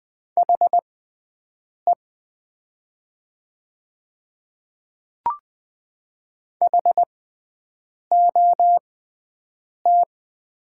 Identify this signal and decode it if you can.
Morse code, heard as a keyed tone, about 20 wpm, spaced as for 7 wpm, 700 Hz. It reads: HE HOT
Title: 6x Character Spacing - Top 100 Words - 20wpm